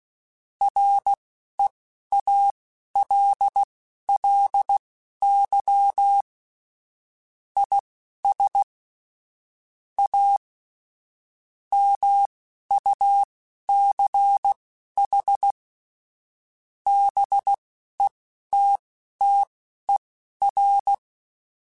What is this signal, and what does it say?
Morse code, heard as a keyed tone, about 16 wpm, spaced as for 11 wpm, 800 Hz. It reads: REALLY IS A MUCH BETTER
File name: Technical Subjects / Technical Subjects-mp3s / Gates Intro @10